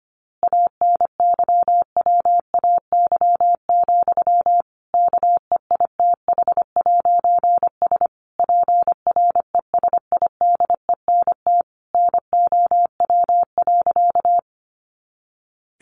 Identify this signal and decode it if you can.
Morse code, heard as a keyed tone, about 25 wpm, 700 Hz. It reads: ANYWAY, KEITH'S PRESIDENT NOW.
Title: A Sound Of Thunder 25wpm